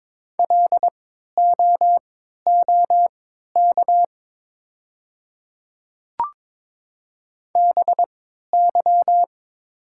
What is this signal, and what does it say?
Morse code, heard as a keyed tone, about 22 wpm, spaced as for 12 wpm, 700 Hz. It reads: LOOK BY
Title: 3x Character Spacing - Top 300 Words - 22wpm